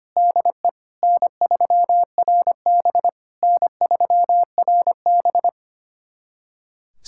Morse code, heard as a keyed tone, about 25 wpm, 700 Hz. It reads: DE N3RB N3RB
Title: SOTA QSO Practice 25wpm